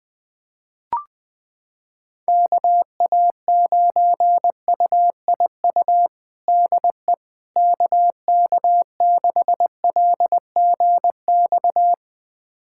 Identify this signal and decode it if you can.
Morse code, heard as a keyed tone, about 21 wpm, 700 Hz. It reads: KA9UIU DE KK6LGX
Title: USA Callsign DE Callsign - 6 Characters - 20wpm